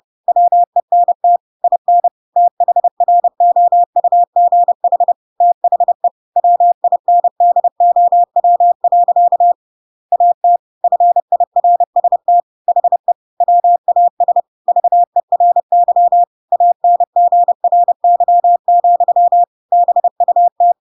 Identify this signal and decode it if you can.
Morse code, heard as a keyed tone, about 30 wpm, 700 Hz. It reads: WENT IN THROUGH THE WINDOW. AT FIRST HE WAS VERY ANGRY, BUT
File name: Pinocchio / Pinocchio ch7 30 30 ik1hge